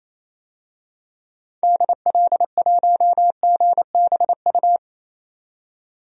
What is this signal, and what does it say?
Morse code, heard as a keyed tone, about 27 wpm, 700 Hz. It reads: DL1GBU
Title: DXCC-100 Call Signs - 6 Characters - 28wpm